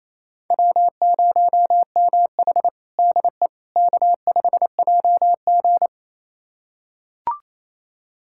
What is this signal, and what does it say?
Morse code, heard as a keyed tone, about 28 wpm, 700 Hz. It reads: W0MH DE K5JG
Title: USA Callsign DE Callsign - 4 Characters - 28wpm